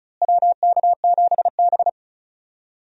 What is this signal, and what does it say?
Morse code, heard as a keyed tone, about 36 wpm, 700 Hz. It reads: WK7B
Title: USA Call Signs - 4 Characters - Repeated 3x - 35wpm